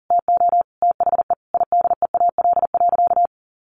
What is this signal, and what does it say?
Morse code, heard as a keyed tone, about 40 wpm, 700 Hz. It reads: TO THE IDEAL.
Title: Sentences from Top 1400 Words 40wpm